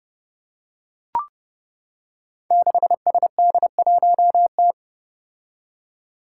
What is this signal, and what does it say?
Morse code, heard as a keyed tone, about 30 wpm, 700 Hz. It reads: 6SD1T
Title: Five Challenging Letter-Number Combinations 30wpm